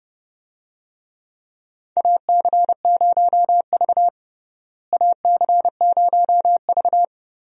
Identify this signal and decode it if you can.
Morse code, heard as a keyed tone, about 32 wpm, 700 Hz. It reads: AC0V AC0V
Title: USA Call Signs - 4 Characters - Repeated 3x - 30wpm